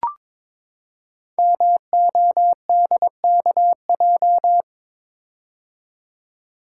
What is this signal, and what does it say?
Morse code, heard as a keyed tone, about 22 wpm, 700 Hz. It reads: MODKJ
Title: Code Groups - 5 Letter:Numbers - 22wpm